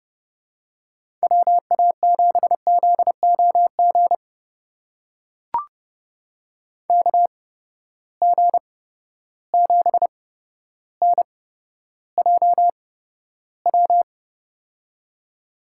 Morse code, heard as a keyed tone, about 30 wpm, spaced as for 13 wpm, 700 Hz. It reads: WA7ZOG KG7NJW
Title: USA Call Signs - 6 Characters - 8x Character Spacing - 30wpm